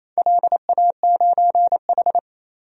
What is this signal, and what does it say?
Morse code, heard as a keyed tone, about 28 wpm, 700 Hz. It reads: LA9H
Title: DXCC-100 Call Signs - 4 Characters - 28wpm